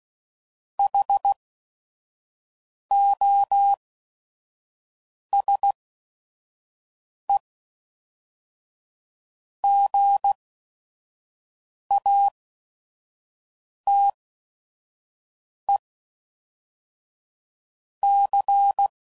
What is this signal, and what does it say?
Morse code, heard as a keyed tone, about 16 wpm, spaced as for 5 wpm, 800 Hz. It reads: HOSE GATE C
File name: CW Code Groups @5WPM / Technical Subjects / AND Gate